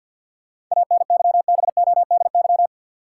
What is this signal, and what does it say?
Morse code, heard as a keyed tone, about 53 wpm, 700 Hz. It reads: ANYBODY
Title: Sets of 5 Words - New Word Drill - 50wpm